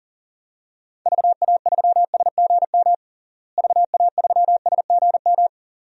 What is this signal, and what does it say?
Morse code, heard as a keyed tone, about 40 wpm, 700 Hz. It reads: VA3SGM VA3SGM
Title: DXCC-100 Call Signs - 6 Characters - Repeated 2x - 40wpm